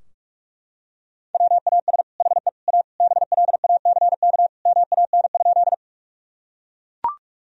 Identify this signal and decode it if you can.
Morse code, heard as a keyed tone, about 41 wpm, 700 Hz. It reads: WAS HE A BLACK MAN?
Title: Sentences from Top 1000 Words - Review Part 2 - 45wpm